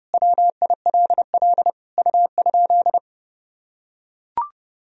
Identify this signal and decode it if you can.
Morse code, heard as a keyed tone, about 30 wpm, 700 Hz. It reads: WILL U?
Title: Sentences from Top 400 Words and Abbreviations 30wpm